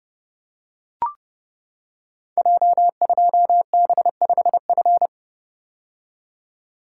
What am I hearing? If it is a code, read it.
Morse code, heard as a keyed tone, about 30 wpm, 700 Hz. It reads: J2B5F
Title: Code Groups - 5 Letter:Numbers - 30wpm